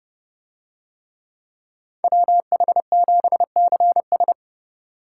Morse code, heard as a keyed tone, about 28 wpm, 700 Hz. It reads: WH7CS